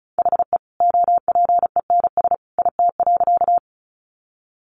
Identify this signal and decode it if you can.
Morse code, heard as a keyed tone, about 34 wpm, 700 Hz. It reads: HE OPENS IT.
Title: Additional Easy Phrases 35wpm